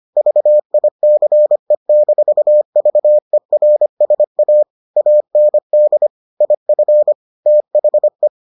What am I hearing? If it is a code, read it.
Morse code, heard as a keyed tone, about 25 wpm, 600 Hz. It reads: VICE-VERSA AND IF THE